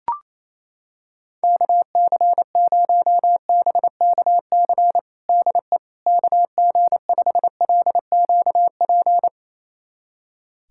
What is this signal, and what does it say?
Morse code, heard as a keyed tone, about 28 wpm, 700 Hz. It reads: KC0BKC DE KG5LQP
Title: USA Callsign DE Callsign - 6 Characters - 28wpm